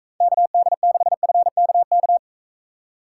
Morse code, heard as a keyed tone, about 44 wpm, 700 Hz. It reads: KD6FKK